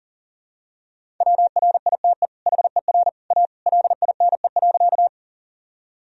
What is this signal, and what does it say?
Morse code, heard as a keyed tone, about 40 wpm, 700 Hz. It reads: WRITE HER A LINE.
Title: Sentences from Top 1000 Words - Review Part 1 - 40wpm